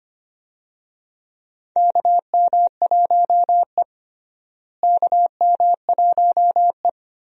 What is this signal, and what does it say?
Morse code, heard as a keyed tone, about 25 wpm, 700 Hz. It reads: KM1E KM1E
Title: USA Call Signs - 4 Characters - Repeated 2x - 25wpm